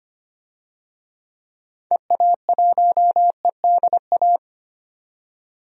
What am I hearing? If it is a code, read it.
Morse code, heard as a keyed tone, about 25 wpm, 700 Hz. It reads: EA1EDA
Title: DXCC-100 Call Signs - 6 Characters - 25wpm